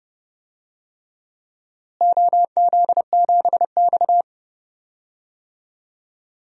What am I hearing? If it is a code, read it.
Morse code, heard as a keyed tone, about 30 wpm, 700 Hz. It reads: OZ7X
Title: DXCC-100 Call Signs - 4 Characters - Repeated 3x - 30wpm